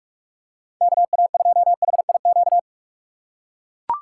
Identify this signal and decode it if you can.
Morse code, heard as a keyed tone, about 42 wpm, 690 Hz. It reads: KA2HIQ
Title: USA Call Signs - 6 Characters - 2x Character Spacing - 45wpm